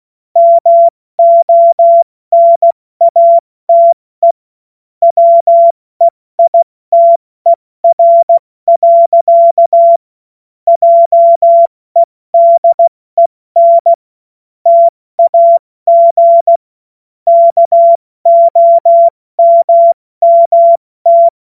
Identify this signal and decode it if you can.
Morse code, heard as a keyed tone, about 16 wpm, 680 Hz. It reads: MONATE WEITER. JEDEN TAG KOMMT
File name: Morsekurs / Witze / Witz 15. November 2022 70 F80